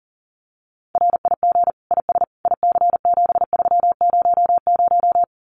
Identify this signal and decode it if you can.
Morse code, heard as a keyed tone, about 40 wpm, 700 Hz. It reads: RIG IS IC7300
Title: Common QSO Phrases Encore 40wpm